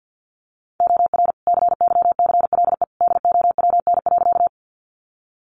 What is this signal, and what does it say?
Morse code, heard as a keyed tone, about 50 wpm, 700 Hz. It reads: OR CYCLE DOWN.
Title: Sentences from Top 900 Words 50wpm